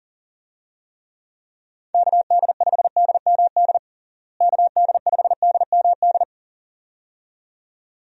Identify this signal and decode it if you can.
Morse code, heard as a keyed tone, about 40 wpm, 700 Hz. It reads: KD5DMD KD5DMD
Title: USA Call Signs - 6 Characters - Repeated 2x - 40wpm